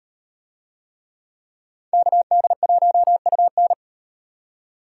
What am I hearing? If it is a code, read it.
Morse code, heard as a keyed tone, about 38 wpm, 700 Hz. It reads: KD1UN